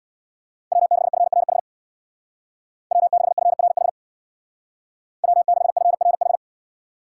Morse code, heard as a keyed tone, about 64 wpm, 700 Hz. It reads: W6FRH W6FRH W6FRH